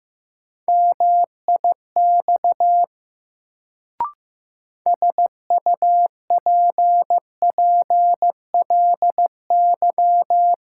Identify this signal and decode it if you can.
Morse code, heard as a keyed tone, about 16 wpm, 700 Hz. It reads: MIX SUPPLY
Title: Sentences from Top 1000 Words - Review Part 4 - New Word Drill - 15wpm